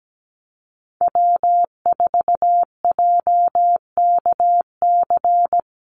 Morse code, heard as a keyed tone, about 17 wpm, 700 Hz. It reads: W4JKC
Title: NA SOTA Chaser Callsigns 17wpm